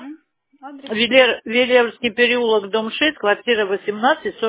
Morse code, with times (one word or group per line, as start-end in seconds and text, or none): none